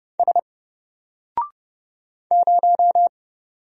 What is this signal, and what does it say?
Morse code, heard as a keyed tone, about 30 wpm, 700 Hz. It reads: S 0
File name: Single-Letter-Number-30wpm